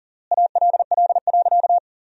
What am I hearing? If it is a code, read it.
Morse code, heard as a keyed tone, about 40 wpm, 700 Hz. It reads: ALL.
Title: 3x Word-Spacing - Sentences from Top 200 Words - 40wpm